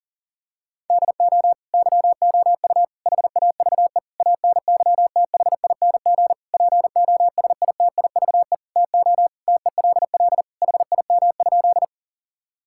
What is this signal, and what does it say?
Morse code, heard as a keyed tone, about 40 wpm, 700 Hz. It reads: DO YOU HAVE ANYTHING POSITIVE TO TELL HIM?
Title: Sentences from Top 1000 Words - Review Part 5 - 40wpm